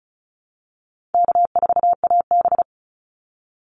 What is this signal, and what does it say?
Morse code, heard as a keyed tone, about 34 wpm, 700 Hz. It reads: K4AB
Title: Call Signs Repeated 2x - 35wpm